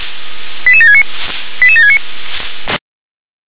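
Five-tone selective call (ZVEI), heard as quick runs of five tones, tones of about 70 ms. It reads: 88769 88769
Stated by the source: Five-tone ZVEI selective call 88725 received by SDR, channel 357.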